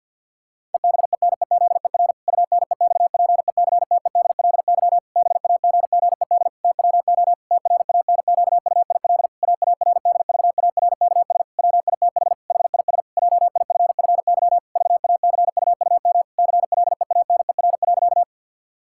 Morse code, heard as a keyed tone, about 50 wpm, 700 Hz. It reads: EBENEZER UNEXPECTEDLY BAGGED TWO TRANQUIL AARDVARKS WITH HIS JIFFY VACUUM CLEANER.
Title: Pangram Challenge v2 - 50wpm